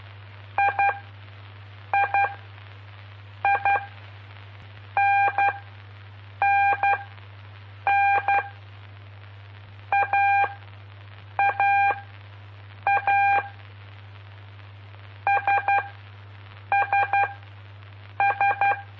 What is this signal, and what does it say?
Morse code, heard as a keyed tone, about 12 wpm, spaced as for 6 wpm, 800 Hz. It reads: III NNN AAA SSS
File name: pp5bar / aulas de telegrafia / Aula-05